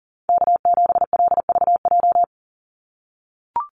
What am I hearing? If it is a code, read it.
Morse code, heard as a keyed tone, about 41 wpm, 700 Hz. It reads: K7LVJ
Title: Call Signs - 40wpm